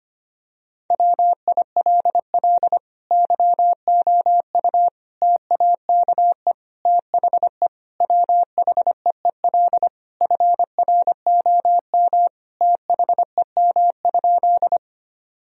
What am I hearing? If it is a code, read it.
Morse code, heard as a keyed tone, about 25 wpm, 700 Hz. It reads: WILL YOU TAKE THE WHEEL FROM THEM?
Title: Sentences from Top 1000 Words - Review Part 3 - 25wpm